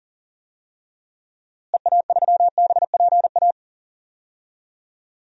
Morse code, heard as a keyed tone, about 40 wpm, 700 Hz. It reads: EA3BPA
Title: DXCC-100 Call Signs - 6 Characters - Repeated 2x - 40wpm